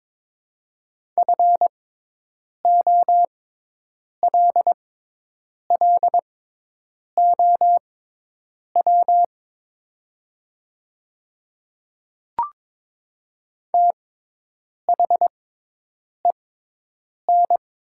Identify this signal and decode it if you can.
Morse code, heard as a keyed tone, about 22 wpm, spaced as for 8 wpm, 700 Hz. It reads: FOLLOW THEN
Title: 6x Character Spacing - Top 300 Words - 22wpm